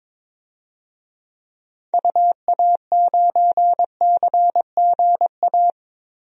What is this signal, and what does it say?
Morse code, heard as a keyed tone, about 22 wpm, 700 Hz. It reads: UA9CGA
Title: DXCC-100 Call Signs - 6 Characters - Repeated 3x - 22wpm